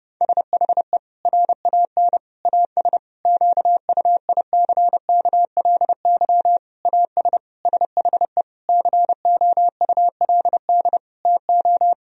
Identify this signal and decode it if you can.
Morse code, heard as a keyed tone, about 30 wpm, 700 Hz. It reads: SHE RAN AS QUICKLY AS SHE COULD TO
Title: Sentences from Top 1000 Words - Review Part 4 - 30wpm